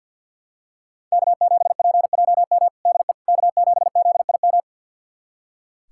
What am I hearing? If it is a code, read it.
Morse code, heard as a keyed tone, about 46 wpm, 700 Hz. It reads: K7PJM DE K7ZIM
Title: USA Callsign DE Callsign - 5 Characters - 50wpm